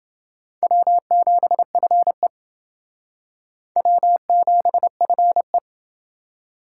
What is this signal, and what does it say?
Morse code, heard as a keyed tone, about 28 wpm, 700 Hz. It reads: W7FE W7FE